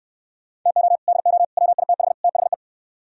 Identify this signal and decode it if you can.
Morse code, heard as a keyed tone, about 66 wpm, 700 Hz. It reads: TO DO WITH THE